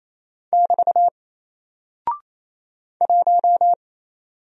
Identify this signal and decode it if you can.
Morse code, heard as a keyed tone, about 27 wpm, 700 Hz. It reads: <BT> 1
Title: Single Character (Letter:Number:Punctuation:Prosign) 28wpm